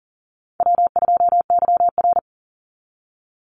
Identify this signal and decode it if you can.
Morse code, heard as a keyed tone, about 42 wpm, 700 Hz. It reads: W2YR
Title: Easy Call Signs - 40wpm